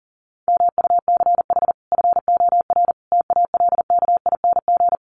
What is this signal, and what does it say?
Morse code, heard as a keyed tone, about 40 wpm, 700 Hz. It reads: MUCH FOR TALKING
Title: Sets of 5 Words Encore 40wpm